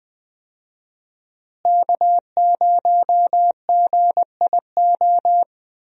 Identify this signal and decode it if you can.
Morse code, heard as a keyed tone, about 20 wpm, 700 Hz. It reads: K0GIO
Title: USA Call Signs - 5 Characters - Repeated 3x - 20wpm